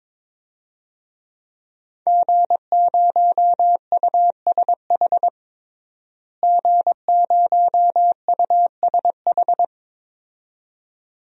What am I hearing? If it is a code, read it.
Morse code, heard as a keyed tone, about 22 wpm, 700 Hz. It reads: G0USH G0USH
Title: DXCC-100 Call Signs - 5 Characters - Repeated 2x - 22wpm